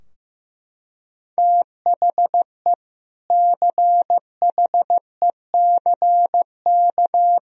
Morse code, heard as a keyed tone, about 16 wpm, 700 Hz. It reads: THE CHECK
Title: Sentences from Top 1000 Words - Review Part 3 - 15wpm